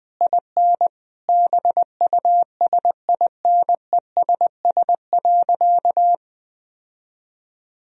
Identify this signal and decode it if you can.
Morse code, heard as a keyed tone, about 21 wpm, 700 Hz. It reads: IN BUSINESS.
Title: Sentences from Top 1000 Words - Review Part 4 - 20wpm